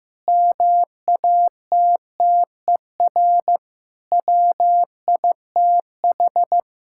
Morse code, heard as a keyed tone, about 16 wpm, 700 Hz. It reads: MATTER WITH
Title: Sentences from Top 1000 Words - Review Part 5 - 15wpm